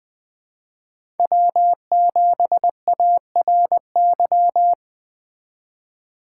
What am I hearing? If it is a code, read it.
Morse code, heard as a keyed tone, about 21 wpm, 700 Hz. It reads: W7ARY
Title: USA Call Signs - 4x Character Spacing - 20wpm